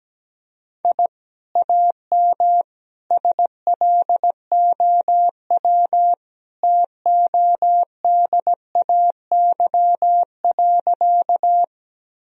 Morse code, heard as a keyed tone, about 17 wpm, 700 Hz. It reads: I AM SLOW TODAY.
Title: Sentences from Top 1000 Words - Review Part 2 - 17wpm